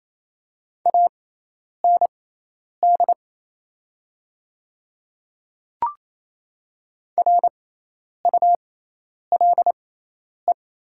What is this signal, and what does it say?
Morse code, heard as a keyed tone, about 28 wpm, spaced as for 10 wpm, 700 Hz. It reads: AND RULE
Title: 6x Character Spacing - Top 500 Words - 28wpm